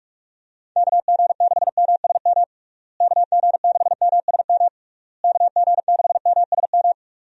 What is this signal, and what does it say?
Morse code, heard as a keyed tone, about 46 wpm, 700 Hz. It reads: KG6MSM KG6MSM KG6MSM
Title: USA Call Signs - 6 Characters - Repeated 3x - 45wpm